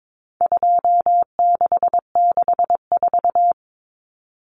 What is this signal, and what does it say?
Morse code, heard as a keyed tone, about 22 wpm, 700 Hz. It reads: 2664
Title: Common Cut Numbers - 4-Digit 22wpm